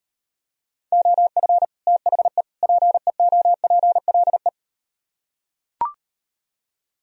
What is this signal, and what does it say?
Morse code, heard as a keyed tone, about 38 wpm, 700 Hz. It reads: OF THE PEOPLE